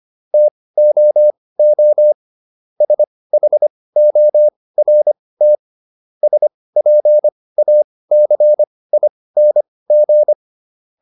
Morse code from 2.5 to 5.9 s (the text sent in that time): SHORT